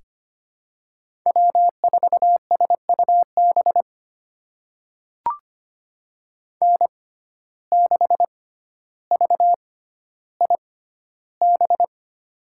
Morse code, heard as a keyed tone, about 25 wpm, spaced as for 13 wpm, 700 Hz. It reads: W4SUB N6VIB
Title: USA Call Signs - 6x Character Spacing - 25wpm